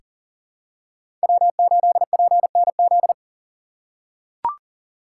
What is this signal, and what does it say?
Morse code, heard as a keyed tone, about 40 wpm, 700 Hz. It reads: W8PNZ
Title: USA Call Signs - 5 Characters - 2x Character Spacing - 40wpm